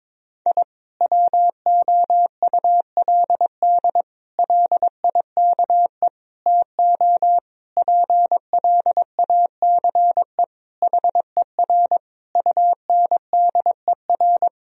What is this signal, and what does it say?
Morse code, heard as a keyed tone, about 22 wpm, 700 Hz. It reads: I WOULD LIKE TO PLACE HER UNDER
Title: Sentences from Top 1000 Words - Review Part 1 - 22wpm